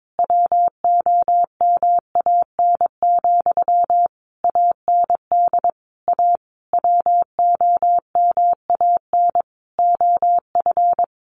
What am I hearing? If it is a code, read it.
Morse code, heard as a keyed tone, about 22 wpm, 700 Hz. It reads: WOMAN, AND A WOMAN OF
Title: Sentences from Top 1400 Words 22wpm